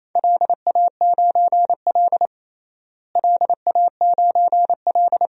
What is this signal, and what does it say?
Morse code, heard as a keyed tone, about 27 wpm, 700 Hz. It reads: LA9L LA9L
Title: DXCC-100 Call Signs - 4 Characters - Repeated 2x - 28wpm